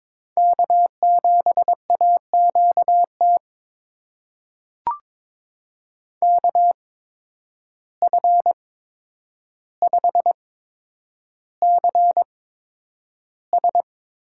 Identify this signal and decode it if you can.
Morse code, heard as a keyed tone, about 22 wpm, spaced as for 9 wpm, 700 Hz. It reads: K7AQT KF5CS
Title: USA Call Signs - 8x Character Spacing - 22wpm